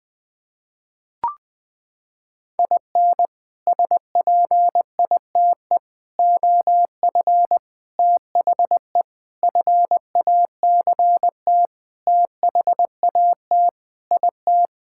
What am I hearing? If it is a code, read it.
Morse code, heard as a keyed tone, about 20 wpm, 700 Hz. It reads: IN SPITE OF THE FACT THAT IT
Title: Sentences with 6-Word Phrases - 20wpm